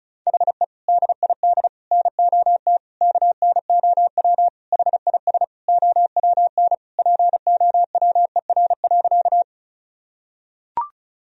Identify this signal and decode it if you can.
Morse code, heard as a keyed tone, about 36 wpm, 700 Hz. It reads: HE DID NOT KNOW HIS OWN POWER.
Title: Sentences from Top 1000 Words - Review Part 2 - 35wpm